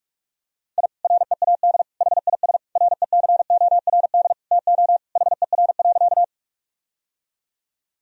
Morse code, heard as a keyed tone, about 41 wpm, 700 Hz. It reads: I READ HIS RECORD TO HER.
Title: Sentences from Top 1000 Words - Review Part 3 - 45wpm